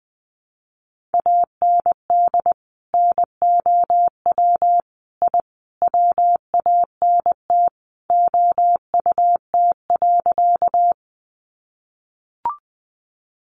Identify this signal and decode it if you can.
Morse code, heard as a keyed tone, about 20 wpm, 700 Hz. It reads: AND NOW I WANT OUT.
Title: A Sound Of Thunder 20wpm